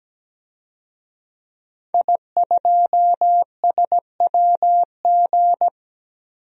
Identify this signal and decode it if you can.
Morse code, heard as a keyed tone, about 17 wpm, 700 Hz. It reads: I2SWG
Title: DXCC-100 Call Signs - 5 Characters - 17wpm